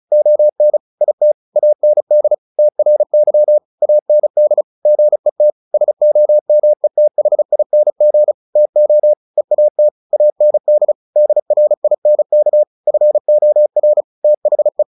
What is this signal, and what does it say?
Morse code, heard as a keyed tone, about 34 wpm, 600 Hz. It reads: ON IT AND TRY AND GET SOMETHING TO EAT AND DRINK FOR THE
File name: the old man and the sea 2 35wpm